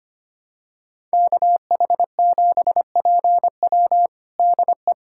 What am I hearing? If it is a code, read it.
Morse code, heard as a keyed tone, about 25 wpm, 700 Hz. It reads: KH7PW DE
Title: USA Callsign DE Callsign - 5 Characters - 25wpm